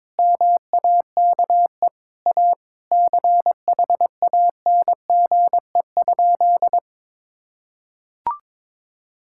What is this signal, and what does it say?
Morse code, heard as a keyed tone, about 22 wpm, 700 Hz. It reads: MAKE A CHANGE?
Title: Sentences from Top 1600 Words 22wpm